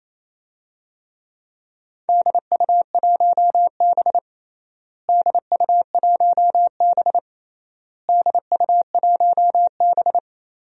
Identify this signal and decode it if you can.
Morse code, heard as a keyed tone, about 28 wpm, 700 Hz. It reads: DU1B DU1B DU1B
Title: DXCC-100 Call Signs - 4 Characters - Repeated 3x - 28wpm